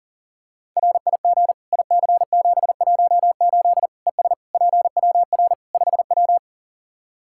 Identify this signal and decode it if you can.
Morse code, heard as a keyed tone, about 40 wpm, 700 Hz. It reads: RIG IC718 ES PWR 5W
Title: Practice QSO - Exchange 2 - Info - 40wpm